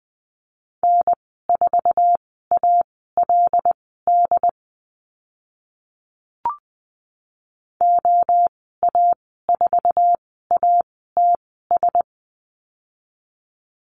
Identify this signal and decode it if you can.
Morse code, heard as a keyed tone, about 20 wpm, spaced as for 14 wpm, 700 Hz. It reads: N4ALD OA4ATS
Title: 076 Lesson 13g - Letter L - Call Signs 20wpm